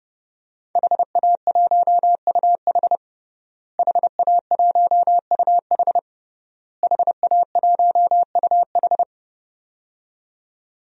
Morse code, heard as a keyed tone, about 30 wpm, 700 Hz. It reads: HA1UH HA1UH HA1UH
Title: DXCC-100 Call Signs - 5 Characters - Repeated 3x - 30wpm